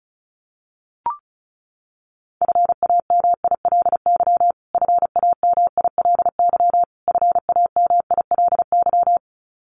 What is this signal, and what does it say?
Morse code, heard as a keyed tone, about 34 wpm, 700 Hz. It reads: FAMILY FAMILY FAMILY
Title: Top 500 Words Repeated 35wpm